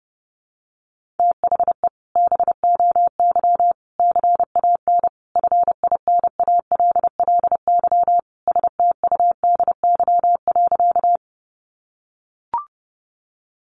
Morse code, heard as a keyed tone, about 30 wpm, 700 Hz. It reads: THE BOY CAN FINALLY STUDY.
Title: Additional Easy Phrases 30wpm